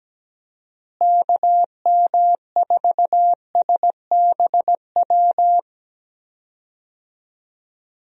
Morse code, heard as a keyed tone, about 17 wpm, 700 Hz. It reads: KM4SBW
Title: USA Call Signs - 6 Characters - Repeated 3x - 17wpm